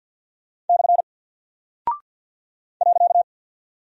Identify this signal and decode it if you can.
Morse code, heard as a keyed tone, about 48 wpm, 700 Hz. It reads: / .
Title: Single Character (Letter:Number:Punctuation:Prosign) Alt 50wpm